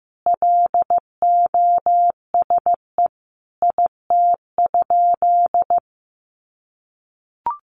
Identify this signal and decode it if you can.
Morse code, heard as a keyed tone, about 15 wpm, 700 Hz. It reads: LOSE IT?
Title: Sentences from Top 1100 Words 15wpm